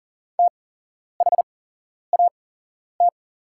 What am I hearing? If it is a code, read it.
Morse code, heard as a keyed tone, about 40 wpm, spaced as for 11 wpm, 700 Hz. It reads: THAT
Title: 8x Character Spacing - Top 500 Words - 40wpm